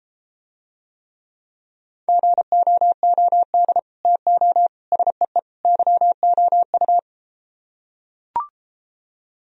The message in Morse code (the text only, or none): GOOD TO SEE YOU